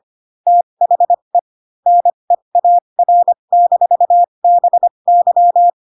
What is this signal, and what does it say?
Morse code, heard as a keyed tone, about 25 wpm, spaced as for 22 wpm, 700 Hz. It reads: THE NEAR-BY